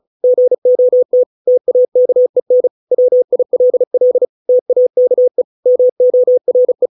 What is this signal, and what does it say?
Morse code, heard as a keyed tone, about 34 wpm, 500 Hz. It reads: GOT TAKEN WILL TAKE MORE